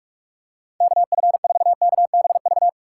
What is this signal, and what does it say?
Morse code, heard as a keyed tone, about 46 wpm, 700 Hz. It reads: KF4KBV
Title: USA Call Signs - 6 Characters - 2x Character Spacing - 45wpm